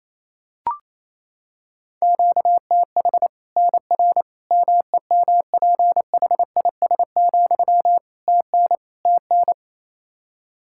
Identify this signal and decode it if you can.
Morse code, heard as a keyed tone, about 28 wpm, 700 Hz. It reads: QTH NR MEMPHIS, TN TN
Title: Practice QSO - Exchange 1 - Beginning - 28wpm